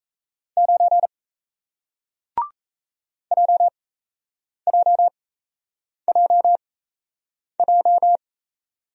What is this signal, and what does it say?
Morse code, heard as a keyed tone, about 37 wpm, spaced as for 8 wpm, 700 Hz. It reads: 9 JJJJ